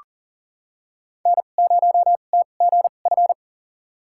Morse code, heard as a keyed tone, about 40 wpm, spaced as for 29 wpm, 700 Hz. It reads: N0TGF
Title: USA Call Signs - 5 Characters - 2x Character Spacing - 40wpm